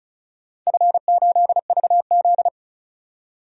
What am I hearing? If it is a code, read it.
Morse code, heard as a keyed tone, about 34 wpm, 700 Hz. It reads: F8VZ
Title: DXCC-100 Call Signs - 4 Characters - 35wpm